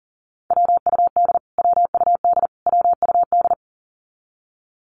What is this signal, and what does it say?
Morse code, heard as a keyed tone, about 40 wpm, 700 Hz. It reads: WUD WUD WUD
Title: Abbreviations Repeated 3x - 40wpm